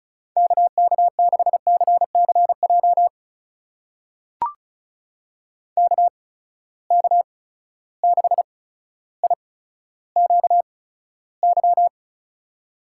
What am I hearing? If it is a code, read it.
Morse code, heard as a keyed tone, about 34 wpm, spaced as for 15 wpm, 700 Hz. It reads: KK6CCJ KK6IQY
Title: USA Call Signs - 6 Characters - 8x Character Spacing - 35wpm